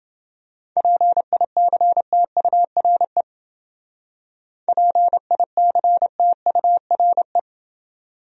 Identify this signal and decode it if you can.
Morse code, heard as a keyed tone, about 28 wpm, 700 Hz. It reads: PICTURE PICTURE